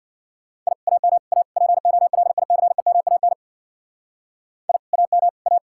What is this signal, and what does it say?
Morse code, heard as a keyed tone, about 64 wpm, 700 Hz. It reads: I AM A POLICEMAN I AM A